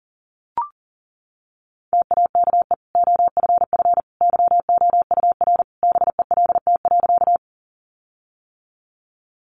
Morse code, heard as a keyed tone, about 40 wpm, 700 Hz. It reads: TAKE OFF YOUR BELT.
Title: Sentences from Top 1400 Words 40wpm